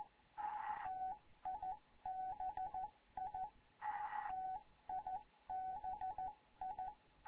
Morse code, heard as a keyed tone, about 14 wpm, 750 Hz. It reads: TIBI TIBI
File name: Morze-HA2MN / hangp / 2 11 GYAKORLAT